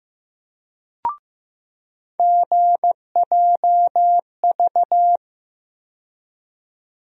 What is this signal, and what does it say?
Morse code, heard as a keyed tone, about 15 wpm, 700 Hz. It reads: GJV
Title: USA Call Sign Postfixes - 3 Letters 15wpm